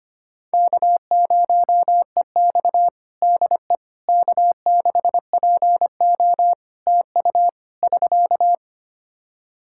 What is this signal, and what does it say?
Morse code, heard as a keyed tone, about 25 wpm, 700 Hz. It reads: K0EX DE K6PO TU <SK>
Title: Practice QSO - Exchange 4 - Ending - 25wpm